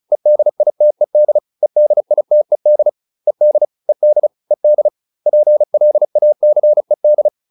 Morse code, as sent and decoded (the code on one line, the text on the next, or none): . -.. .. - . -.. / . -.. .. - . -.. / . -.. / . -.. / . -.. / .--. .-.. .- -.-. -..
EDITED EDITED ED ED ED PLACED